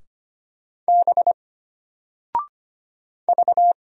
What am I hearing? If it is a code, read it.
Morse code, heard as a keyed tone, about 25 wpm, 700 Hz. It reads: B V
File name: Single-Letter-Number-25wpm